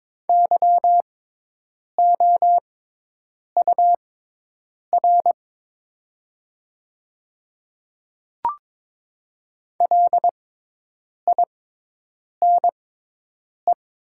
Morse code, heard as a keyed tone, about 22 wpm, spaced as for 8 wpm, 700 Hz. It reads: YOUR LINE
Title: 6x Character Spacing - Top 200 Words - 22wpm